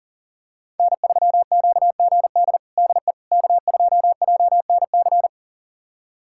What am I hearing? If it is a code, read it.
Morse code, heard as a keyed tone, about 40 wpm, 700 Hz. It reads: N3QGD DE K2JNC
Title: USA Callsign DE Callsign - 5 Characters - 40wpm